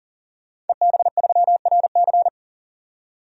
Morse code, heard as a keyed tone, about 37 wpm, 700 Hz. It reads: EB3RC
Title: DXCC-100 Call Signs - 5 Characters - 40wpm